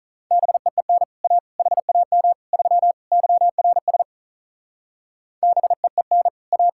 Morse code, heard as a keyed tone, about 45 wpm, 700 Hz. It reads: BEEN A HAM 3 YRS BEEN A